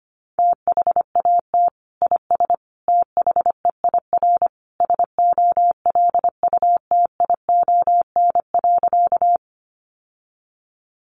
Morse code, heard as a keyed tone, about 25 wpm, 700 Hz. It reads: THAT IS THEIR SOLUTION.
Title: Sentences from Top 1300 Words 25wpm